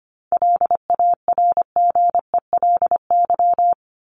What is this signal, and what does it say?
Morse code, heard as a keyed tone, about 25 wpm, 700 Hz. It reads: LARGELY
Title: LY Suffix - Top 100 Words with Base Words 25wpm